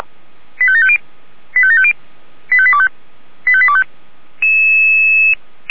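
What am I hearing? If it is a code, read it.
Five-tone selective call (ZVEI), heard as quick runs of five tones, tones of about 70 ms. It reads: 85750 85750 85825 85825
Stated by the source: ZVEI five-tone selective call 85827 received on an SDR.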